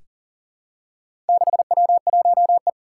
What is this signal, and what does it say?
Morse code, heard as a keyed tone, about 39 wpm, 700 Hz. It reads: 6W1E
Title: Four Challenging Letter-Number Combinations 40wpm